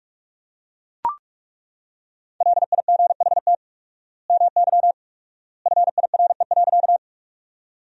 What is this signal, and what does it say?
Morse code, heard as a keyed tone, about 41 wpm, 700 Hz. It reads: LIGHT MY FIRE.
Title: 4x Word-Spacing - Sentences from Top 400 Words - 45wpm